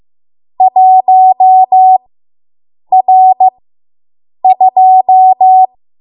Morse code, heard as a keyed tone, about 15 wpm, spaced as for 7 wpm, 750 Hz. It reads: 1R2